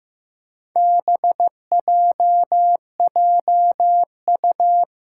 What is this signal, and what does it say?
Morse code, heard as a keyed tone, about 15 wpm, 700 Hz. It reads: BJJU
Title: Four Challenging Letter-Number Combinations 15wpm